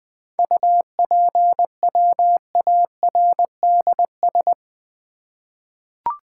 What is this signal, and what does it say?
Morse code, heard as a keyed tone, about 20 wpm, 700 Hz. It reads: UPWARDS
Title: 7 Letter Words - First 3 Letters Given - 20wpm